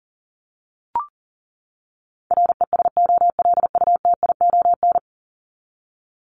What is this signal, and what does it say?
Morse code, heard as a keyed tone, about 40 wpm, 700 Hz. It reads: RESOLUTION
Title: Top 1501-1750 Words 40wpm